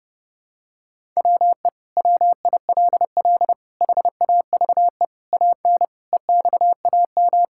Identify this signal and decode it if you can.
Morse code, heard as a keyed tone, about 30 wpm, 700 Hz. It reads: WE WILL HAVE AN EXAM
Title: Sentences from Top 1000 Words - Review Part 5 - 30wpm